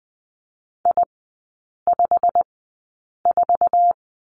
Morse code, heard as a keyed tone, about 20 wpm, spaced as for 8 wpm, 700 Hz. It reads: I54
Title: 087 Lesson 15e - Number 5 - Three Characters 20wpm